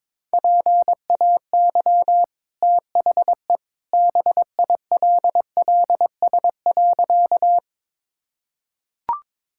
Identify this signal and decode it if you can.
Morse code, heard as a keyed tone, about 22 wpm, 700 Hz. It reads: PAY THE BILLS.
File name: practice / 22wpm / Easy Sentences with Hints - 22wpm